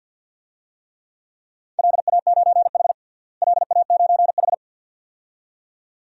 Morse code, heard as a keyed tone, about 47 wpm, 700 Hz. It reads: LA9H LA9H
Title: DXCC-100 Call Signs - 4 Characters - Repeated 2x - 50wpm